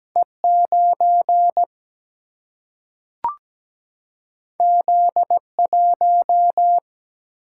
Morse code, heard as a keyed tone, about 17 wpm, 700 Hz. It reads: E9 Z1
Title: Single-Letter Single-Number 17wpm